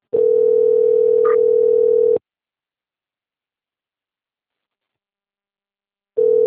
Ringback tone (North America), 440 + 480 Hz.
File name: out- - - - - .27330